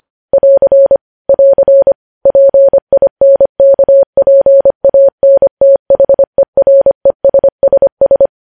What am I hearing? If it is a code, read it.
Morse code, heard as a keyed tone, about 25 wpm, 550 Hz. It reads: <AR> <AR> PINKPANTHERESSS
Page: News Headlines in Morse Code at 25 WPM